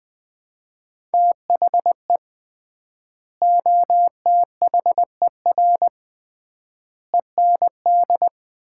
Morse code, heard as a keyed tone, about 21 wpm, 700 Hz. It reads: THE OTHER END
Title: 3x Word-Spacing - Sets of 5 Words - 20wpm